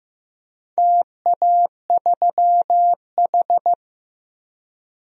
Morse code, heard as a keyed tone, about 15 wpm, 700 Hz. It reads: TA3H
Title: DXCC-100 Call Signs - 4 Characters - Repeated 3x - 15wpm